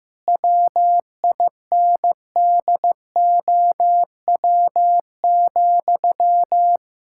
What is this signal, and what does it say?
Morse code, heard as a keyed tone, about 16 wpm, 700 Hz. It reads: WINDOW,
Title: Sentences from Top 1600 Words 15wpm